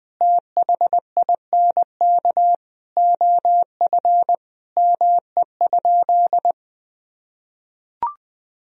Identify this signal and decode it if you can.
Morse code, heard as a keyed tone, about 20 wpm, 700 Hz. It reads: THINK OF ME?
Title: Sentences from Top 1000 Words - Review Part 3 - 20wpm